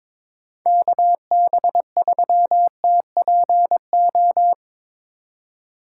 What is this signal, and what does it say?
Morse code, heard as a keyed tone, about 24 wpm, 700 Hz. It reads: KB3TPO
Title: Speed Racing - USA 2x3 Call Signs 22wpm